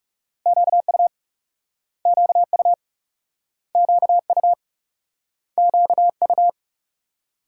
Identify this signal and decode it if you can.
Morse code, heard as a keyed tone, about 37 wpm, 700 Hz. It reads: QU QU QU QU
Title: Speed-Racing Multi-letter Phonemes - 30wpm